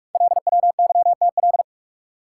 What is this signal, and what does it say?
Morse code, heard as a keyed tone, about 44 wpm, 700 Hz. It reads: LWYTL
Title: Code Groups - 5 Letters - 45wpm